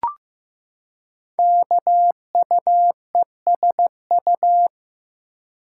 Morse code, heard as a keyed tone, about 15 wpm, 700 Hz. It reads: KUESU
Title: Code Groups - 5 Letter:Numbers - 15wpm